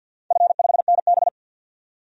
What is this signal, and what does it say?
Morse code, heard as a keyed tone, about 50 wpm, 700 Hz. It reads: F5NB
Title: DXCC-100 Call Signs - 4 Characters - 50wpm